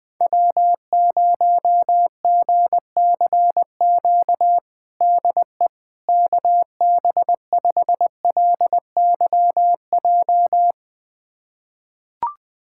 Morse code, heard as a keyed tone, about 20 wpm, 700 Hz. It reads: W0GCQ DE KB5LYJ
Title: USA Callsign DE Callsign 20wpm